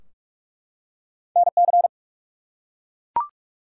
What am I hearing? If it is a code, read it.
Morse code, heard as a keyed tone, about 44 wpm, 700 Hz. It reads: NC
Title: US States and Canadian Provinces 45wpm